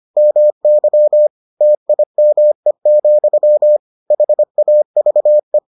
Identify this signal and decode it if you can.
Morse code, heard as a keyed tone, about 25 wpm, 600 Hz. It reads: MY TIME, HAVE